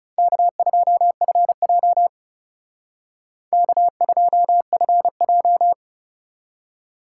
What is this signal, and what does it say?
Morse code, heard as a keyed tone, about 36 wpm, 700 Hz. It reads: K2FJ K2FJ
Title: Speed Racing - USA 1x2 Call Signs 30wpm